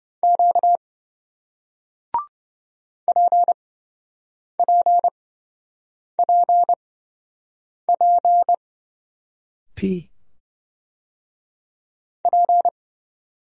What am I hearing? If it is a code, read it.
Morse code, heard as a keyed tone, about 27 wpm, spaced as for 7 wpm, 700 Hz. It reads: Q PPPP P